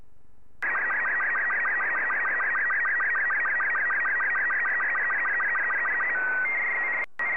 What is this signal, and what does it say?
SSTV (slow-scan television), Robot 36.